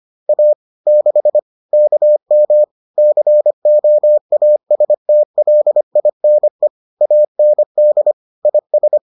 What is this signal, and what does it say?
Morse code, heard as a keyed tone, about 25 wpm, 600 Hz. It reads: A 6 KM COASTLINE AND IS